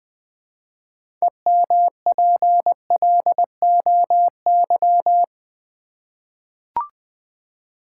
Morse code, heard as a keyed tone, about 20 wpm, 700 Hz. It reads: EMPLOY
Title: Six Letter Words - 20wpm